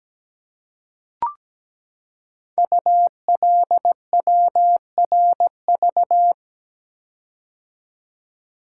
Morse code, heard as a keyed tone, about 17 wpm, 700 Hz. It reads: ULWRV
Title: Code Groups - 5 Letters - 17wpm